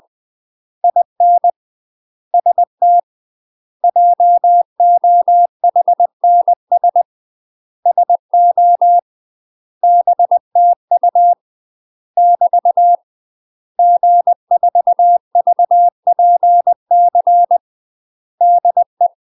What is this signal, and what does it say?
Morse code, heard as a keyed tone, about 20 wpm, 700 Hz. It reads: IN ST JOHNS SO BTU <BT> G4VPC DE
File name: UK MASTER QSOs TRAINING / 20WPM / UK MASTER QSOs 0035